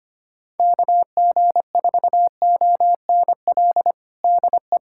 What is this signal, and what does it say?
Morse code, heard as a keyed tone, about 25 wpm, 700 Hz. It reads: KG4ONL DE
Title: USA Callsign DE Callsign - 6 Characters - 25wpm